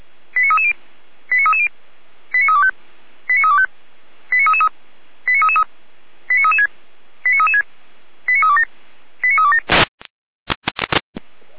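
Five-tone selective call (ZVEI), heard as quick runs of five tones, tones of about 70 ms. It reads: 89330 89330 89325 89325 89333 89333 89336 89336 89327 89327
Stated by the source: ZVEI five-tone selective call received by SDR, from the audio server of Feuerwehr Orlamünde (fire brigade).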